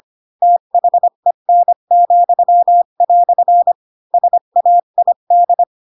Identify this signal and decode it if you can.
Morse code, heard as a keyed tone, about 25 wpm, 700 Hz. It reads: THEN," SAID
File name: Pinocchio / Pinocchio ch7 20 25 ik1hge